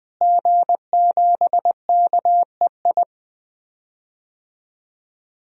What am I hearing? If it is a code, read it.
Morse code, heard as a keyed tone, about 20 wpm, 700 Hz. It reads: G7KEI
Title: DXCC-100 Call Signs - 5 Characters - Repeated 3x - 20wpm